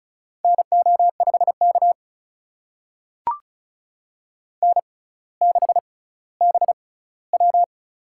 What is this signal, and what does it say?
Morse code, heard as a keyed tone, about 36 wpm, spaced as for 18 wpm, 700 Hz. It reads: NO5K N6BW
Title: USA Call Signs - 4 Characters - 6x Character Spacing - 35wpm